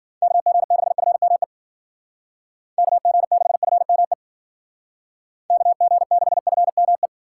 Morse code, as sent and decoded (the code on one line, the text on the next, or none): -.- --. -.... ..-. -- . / -.- --. -.... ..-. -- . / -.- --. -.... ..-. -- .
KG6FME KG6FME KG6FME